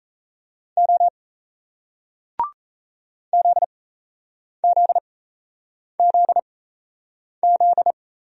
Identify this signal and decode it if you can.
Morse code, heard as a keyed tone, about 37 wpm, spaced as for 8 wpm, 700 Hz. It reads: O ZZZZ